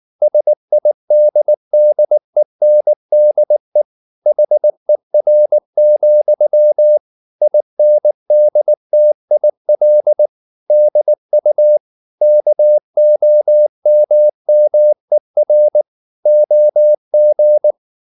Morse code, heard as a keyed tone, about 19 wpm, 600 Hz. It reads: SIDDENDE HER, INDTIL DU KOMMER OG
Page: Ferie 19 wpm